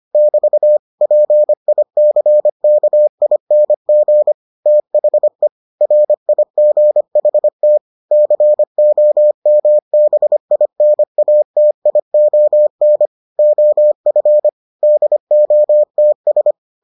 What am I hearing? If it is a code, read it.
Morse code, heard as a keyed tone, about 25 wpm, 600 Hz. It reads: <BT> PICKING THE RIGHT COMBINATION OF DOTS